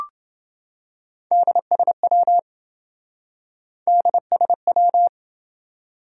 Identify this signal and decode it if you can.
Morse code, heard as a keyed tone, about 27 wpm, 700 Hz. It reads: DSW DSW